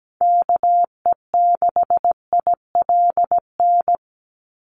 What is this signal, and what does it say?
Morse code, heard as a keyed tone, about 17 wpm, 700 Hz. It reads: KE6ILN
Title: USA Call Signs 17wpm